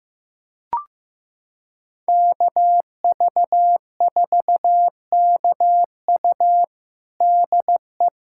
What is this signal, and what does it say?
Morse code, heard as a keyed tone, about 15 wpm, 700 Hz. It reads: KV4KU DE
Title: USA Callsign DE Callsign 15wpm